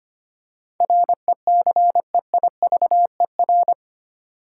Morse code, heard as a keyed tone, about 25 wpm, 700 Hz. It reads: RECEIVER
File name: Eight Letter Words - 25wpm